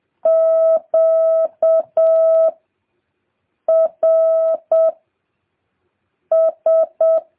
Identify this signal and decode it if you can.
Morse code, heard as a keyed tone, about 7 wpm, 650 Hz. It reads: Q R S